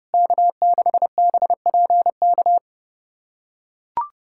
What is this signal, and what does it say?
Morse code, heard as a keyed tone, about 30 wpm, 700 Hz. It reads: K6BPK
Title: USA Callsign DE Callsign - 5 Characters - 30wpm